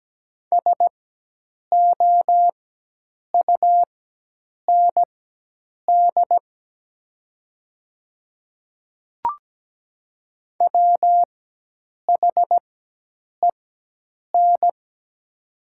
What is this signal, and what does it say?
Morse code, heard as a keyed tone, about 17 wpm, spaced as for 8 wpm, 700 Hz. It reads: SOUND WHEN
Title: 4x Character Spacing - Top 100 Words - 17wpm